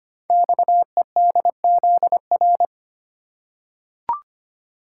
Morse code, heard as a keyed tone, about 25 wpm, 700 Hz. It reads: XEDZR
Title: Code Groups - 5 Letters - 25wpm